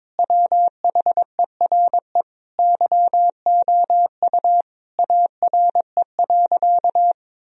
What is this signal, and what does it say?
Morse code, heard as a keyed tone, about 21 wpm, 700 Hz. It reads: WHERE YOU ARE.
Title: Sentences from Top 1000 Words - Review Part 1 - 22wpm